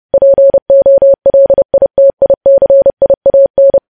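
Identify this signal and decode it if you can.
Morse code, heard as a keyed tone, about 30 wpm, 550 Hz. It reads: POLITICIAN
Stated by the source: Morse news headlines sent at 30 wpm.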